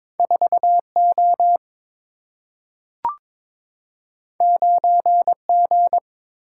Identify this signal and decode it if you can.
Morse code, heard as a keyed tone, about 22 wpm, 700 Hz. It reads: 4 O 9 G
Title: Single-Letter Single-Number 22wpm